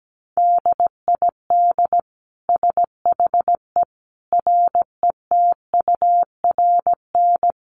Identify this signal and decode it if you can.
Morse code, heard as a keyed tone, about 17 wpm, 700 Hz. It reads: DID SHE RETURN
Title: Sentences from Top 800 Words 17wpm